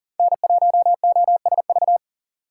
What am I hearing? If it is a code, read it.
Morse code, heard as a keyed tone, about 38 wpm, 700 Hz. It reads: N1OSV
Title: USA Call Signs - 5 Characters - 3x Character Spacing - 40wpm